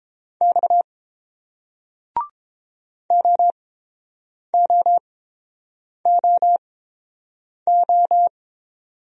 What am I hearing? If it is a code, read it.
Morse code, heard as a keyed tone, about 29 wpm, spaced as for 7 wpm, 690 Hz. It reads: X OOOO